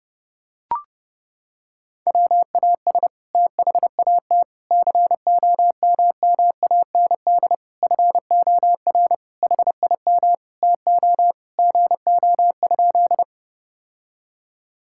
Morse code, heard as a keyed tone, about 30 wpm, 700 Hz. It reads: WAS THAT COMMAND FOR HIM TO GO?
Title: Sentences from Top 1600 Words 30wpm